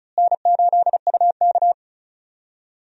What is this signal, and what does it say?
Morse code, heard as a keyed tone, about 36 wpm, 700 Hz. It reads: N8UK
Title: USA Call Signs - 4 Characters - Repeated 3x - 35wpm